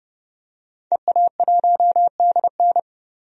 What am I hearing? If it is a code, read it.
Morse code, heard as a keyed tone, about 30 wpm, 700 Hz. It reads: EA1DN
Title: DXCC-100 Call Signs - 5 Characters - 30wpm